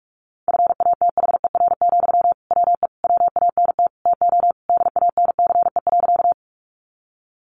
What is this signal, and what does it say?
Morse code, heard as a keyed tone, about 44 wpm, 700 Hz. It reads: FATHER, WE WANT TO DANCE.
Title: Sentences from Top 700 Words 45wpm